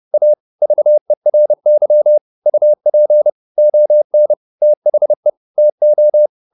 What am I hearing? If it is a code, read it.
Morse code, heard as a keyed tone, about 30 wpm, 600 Hz. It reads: A VERY UP ON THE TO